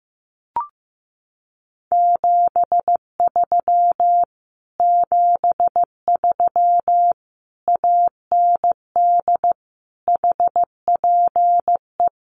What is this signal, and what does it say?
Morse code, heard as a keyed tone, about 14 wpm, 700 Hz. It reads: 73 73 AND HPE
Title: Common QSO Phrases Encore 15wpm